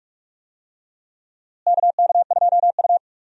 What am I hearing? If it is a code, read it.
Morse code, heard as a keyed tone, about 42 wpm, 700 Hz. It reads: KK2U
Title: USA Call Signs - 4 Characters - Repeated 2x - 45wpm